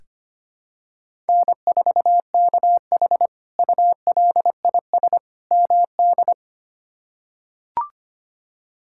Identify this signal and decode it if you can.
Morse code, heard as a keyed tone, about 25 wpm, 700 Hz. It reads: N4KH ULIS MD
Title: K1USN SST QSO Practice - 25wpm